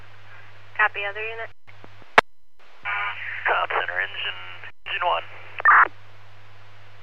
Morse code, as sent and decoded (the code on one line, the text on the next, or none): none